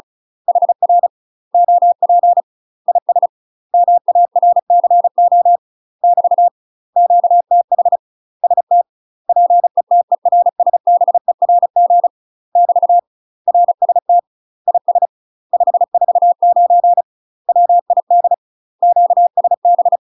Morse code, heard as a keyed tone, about 36 wpm, 700 Hz. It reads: HR OP IS MARCO <BT> QTH ST PETERSBERG <BT> RST IS 549 WID QSB
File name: UK MASTER QSOs TRAINING / 35WPM / UK MASTER QSOs 0041